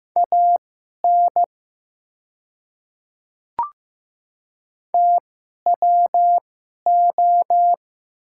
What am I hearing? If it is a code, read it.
Morse code, heard as a keyed tone, about 15 wpm, spaced as for 11 wpm, 700 Hz. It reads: AN TWO
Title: 2x Character Spacing - Top 100 Words - 15wpm